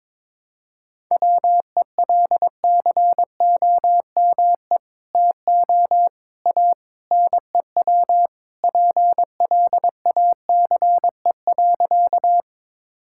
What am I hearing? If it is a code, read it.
Morse code, heard as a keyed tone, about 22 wpm, 700 Hz. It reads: WELCOME TO A NEW PLACE.
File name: practice / 22wpm / Sentences with 2-Word Phrases - 22wpm